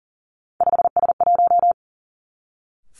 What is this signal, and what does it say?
Morse code, heard as a keyed tone, about 42 wpm, 700 Hz. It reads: 5S1
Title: Three Challenging Letter-Number Combinations 40wpm